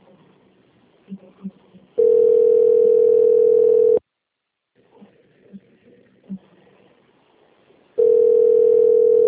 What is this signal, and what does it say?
Ringback tone (North America), 440 + 480 Hz.
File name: out- - - - - .37594